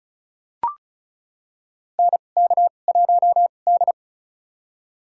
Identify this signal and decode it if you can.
Morse code, heard as a keyed tone, about 34 wpm, spaced as for 25 wpm, 700 Hz. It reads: NK1D
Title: USA Call Signs - 4 Characters - 2x Character Spacing - 35wpm